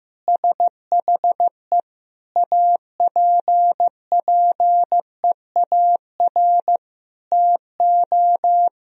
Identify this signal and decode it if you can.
Morse code, heard as a keyed tone, about 15 wpm, 700 Hz. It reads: SHE APPEAR TO
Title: Sentences from Top 1000 Words - Review Part 2 - 15wpm